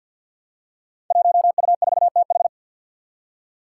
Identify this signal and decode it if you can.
Morse code, heard as a keyed tone, about 49 wpm, 700 Hz. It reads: 1U4TH
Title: Five Challenging Letter-Number Combinations 50wpm